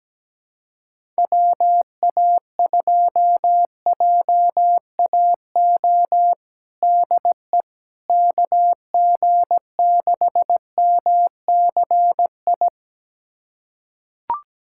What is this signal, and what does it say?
Morse code, heard as a keyed tone, about 17 wpm, 700 Hz. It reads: WA2JAO DE KG6MCI
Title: USA Callsign DE Callsign 17wpm